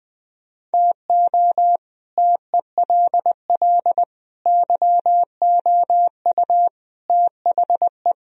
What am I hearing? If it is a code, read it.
Morse code, heard as a keyed tone, about 20 wpm, 700 Hz. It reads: TO TELL YOU THE